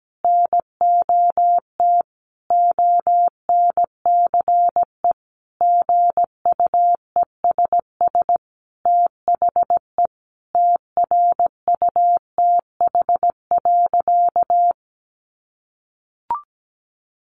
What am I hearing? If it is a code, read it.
Morse code, heard as a keyed tone, about 17 wpm, 700 Hz. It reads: NOT ONCE GUESS THE TRUTH.
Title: Sentences from Top 1500 Words 17wpm